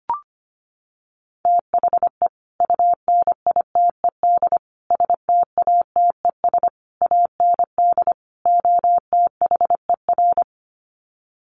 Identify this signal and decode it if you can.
Morse code, heard as a keyed tone, about 25 wpm, 700 Hz. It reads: THE UNITED STATES AND OTHER
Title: Sets of 5 Words Encore 25wpm